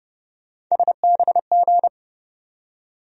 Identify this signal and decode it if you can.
Morse code, heard as a keyed tone, about 29 wpm, 700 Hz. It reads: SBG
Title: USA Call Sign Postfixes - 3 Letters 30wpm